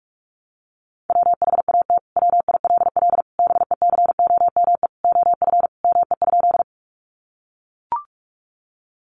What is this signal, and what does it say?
Morse code, heard as a keyed tone, about 44 wpm, 700 Hz. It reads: WHAT WILL BECOME OF ME?
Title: Sentences from Top 600 Words 45wpm